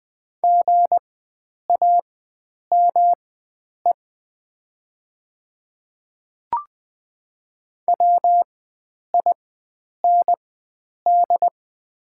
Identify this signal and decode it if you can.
Morse code, heard as a keyed tone, about 20 wpm, spaced as for 9 wpm, 700 Hz. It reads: GAME WIND
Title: 4x Character Spacing - Top 500 Words - 20wpm